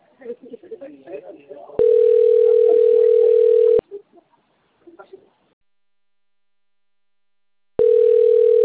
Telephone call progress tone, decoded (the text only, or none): ringback tone